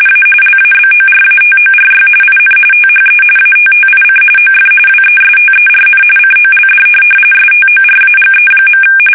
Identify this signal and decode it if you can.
RTTY, about 45.45 baud. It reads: A LABORATORY IN THE ATTIC OF WESTERN COLD STORAGE.
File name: fsk850 / teletype-corp